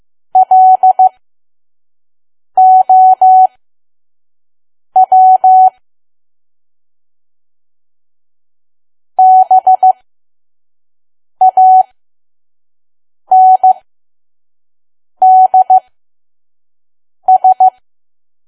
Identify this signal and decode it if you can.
Morse code, heard as a keyed tone, about 15 wpm, spaced as for 5 wpm, 750 Hz. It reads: LOW BANDS
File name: Archive / 5 WPM files / PM